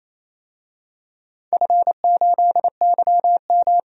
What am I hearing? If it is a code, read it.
Morse code, heard as a keyed tone, about 28 wpm, 700 Hz. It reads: F8YM
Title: DXCC-100 Call Signs - 4 Characters - 28wpm